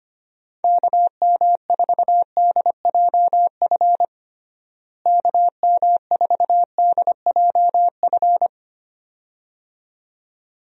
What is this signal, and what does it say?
Morse code, heard as a keyed tone, about 25 wpm, 700 Hz. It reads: KM4DJF KM4DJF
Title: USA Call Signs - 6 Characters - Repeated 2x - 25wpm